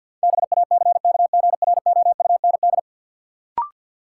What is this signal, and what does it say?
Morse code, heard as a keyed tone, about 52 wpm, 700 Hz. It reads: BACKGROUND
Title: Sentences from Top 1300 Words - New Word Drill Alt - 50wpm